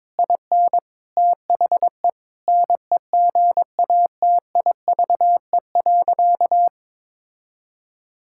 Morse code, heard as a keyed tone, about 22 wpm, 700 Hz. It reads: IN THE NEGATIVE.
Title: Sentences from Top 1000 Words - Review Part 6 - 22wpm